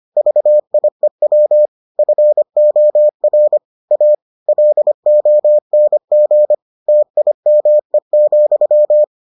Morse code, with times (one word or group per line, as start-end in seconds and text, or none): 0.2-1.7 VIEW
2.0-3.6 FOR
3.9-4.2 A
4.5-6.5 LONG
6.9-9.0 TIME,